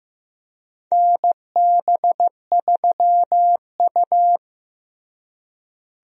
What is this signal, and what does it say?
Morse code, heard as a keyed tone, about 15 wpm, 700 Hz. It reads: NB3U
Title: USA Call Signs - 4 Characters - Repeated 2x - 15wpm